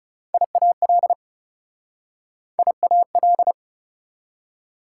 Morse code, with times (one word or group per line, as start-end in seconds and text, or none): 0.3-1.1 IAL
2.6-3.5 IAL